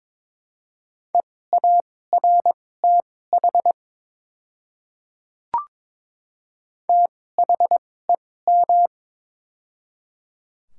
Morse code, heard as a keyed tone, about 22 wpm, spaced as for 16 wpm, 700 Hz. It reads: EARTH THEM
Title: 2x Character Spacing - Top 200 Words - 22wpm